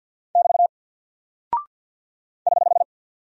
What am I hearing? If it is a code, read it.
Morse code, heard as a keyed tone, about 48 wpm, 700 Hz. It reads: <BT> <HH>
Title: Single Character (Letter:Number:Punctuation:Prosign) Alt 50wpm